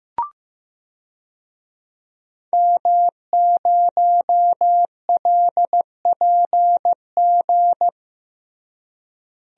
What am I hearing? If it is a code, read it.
Morse code, heard as a keyed tone, about 15 wpm, 700 Hz. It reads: M0LPG
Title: DXCC-100 Call Signs - 5 Characters - Repeated 2x - 15wpm